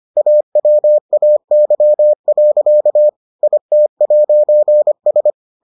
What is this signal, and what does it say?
Morse code, heard as a keyed tone, about 25 wpm, 600 Hz. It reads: AWAY. IT'S